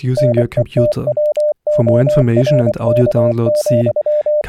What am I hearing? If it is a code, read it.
Morse code, heard as a keyed tone, about 24 wpm, 600 Hz, with speech. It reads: NETWORK,